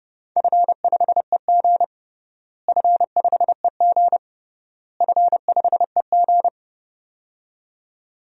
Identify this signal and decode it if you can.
Morse code, heard as a keyed tone, about 30 wpm, 700 Hz. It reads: F5EG F5EG F5EG
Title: DXCC-100 Call Signs - 4 Characters - Repeated 3x - 30wpm